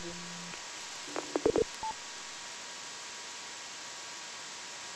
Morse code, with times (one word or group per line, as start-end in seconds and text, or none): none